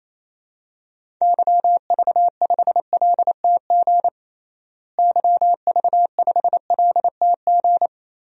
Morse code, heard as a keyed tone, about 28 wpm, 700 Hz. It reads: YV5LTG YV5LTG
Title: DXCC-100 Call Signs - 6 Characters - Repeated 3x - 28wpm